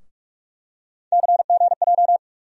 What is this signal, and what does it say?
Morse code, heard as a keyed tone, about 46 wpm, 700 Hz. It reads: CGJ
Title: USA Call Sign Postfixes - 3 Letters 45wpm